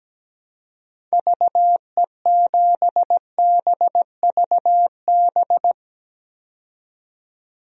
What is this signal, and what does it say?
Morse code, heard as a keyed tone, about 17 wpm, 700 Hz. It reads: VE7BVB
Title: DXCC-100 Call Signs - 6 Characters - 17wpm